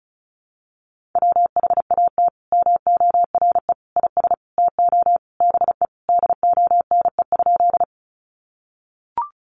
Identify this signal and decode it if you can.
Morse code, heard as a keyed tone, about 34 wpm, 700 Hz. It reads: WHAT MORE IS TO BE DONE?
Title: Sentences from Top 500 Words 35wpm